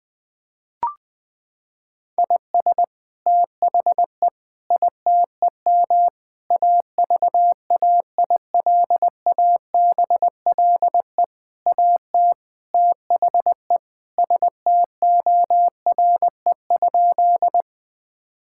Morse code, heard as a keyed tone, about 21 wpm, 700 Hz. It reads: IS THE ITEM AVAILABLE AT THE STORE?
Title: Sentences from Top 1000 Words - Review Part 6 - 20wpm